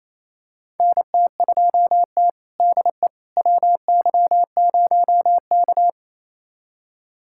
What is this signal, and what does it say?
Morse code, heard as a keyed tone, about 28 wpm, 700 Hz. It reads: NT2T DE WY0K
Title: USA Callsign DE Callsign - 4 Characters - 28wpm